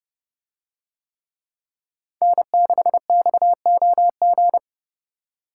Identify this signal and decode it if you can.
Morse code, heard as a keyed tone, about 29 wpm, 700 Hz. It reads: N6XOG